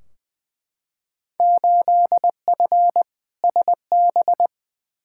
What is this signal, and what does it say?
Morse code, heard as a keyed tone, about 20 wpm, 700 Hz. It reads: 8F SB
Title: Field Day Exchanges Encore - 20wpm